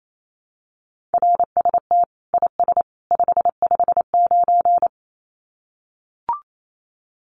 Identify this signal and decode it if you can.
Morse code, heard as a keyed tone, about 28 wpm, 700 Hz. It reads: RST IS 559
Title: Common QSO Phrases 28wpm